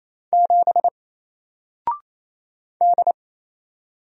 Morse code, heard as a keyed tone, about 28 wpm, 700 Hz. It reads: 7 D
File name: Single-Letter-Number-28wpm